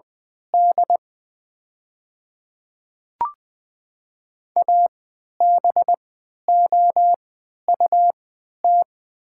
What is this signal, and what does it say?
Morse code, heard as a keyed tone, about 20 wpm, spaced as for 11 wpm, 700 Hz. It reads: D ABOUT
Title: 3x Character Spacing - Top 200 Words - 20wpm